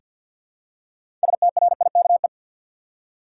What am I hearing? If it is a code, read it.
Morse code, heard as a keyed tone, about 50 wpm, 700 Hz. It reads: STRIKE
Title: Six Letter Words - 50wpm